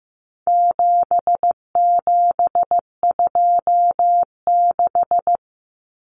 Morse code, heard as a keyed tone, about 15 wpm, 700 Hz. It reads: 7726
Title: Numbers - 4-Digit 15wpm